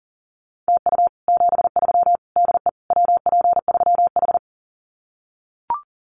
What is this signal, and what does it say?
Morse code, heard as a keyed tone, about 40 wpm, 700 Hz. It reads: TU 73 DE WP3H
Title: Common QSO Phrases Encore 40wpm